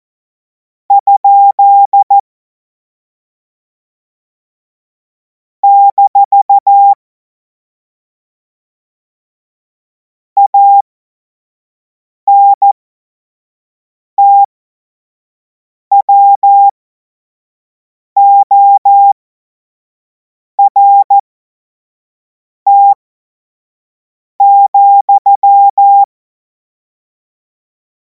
Morse code, heard as a keyed tone, about 14 wpm, spaced as for 5 wpm, 800 Hz. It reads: ? - ANTWORT,
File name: Morsekurs / Witze / Witz 26. Mai 2026 25 F70